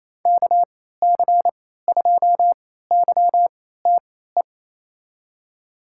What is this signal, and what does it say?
Morse code, heard as a keyed tone, about 28 wpm, spaced as for 16 wpm, 700 Hz. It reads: KC2YTE
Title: USA Call Signs - 3x Character Spacing - 28wpm